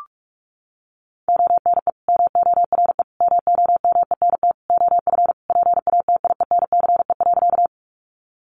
Morse code, heard as a keyed tone, about 44 wpm, 700 Hz. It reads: ONE MORE MOMENT OF PATIENCE.
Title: Sentences from Top 1300 Words 45wpm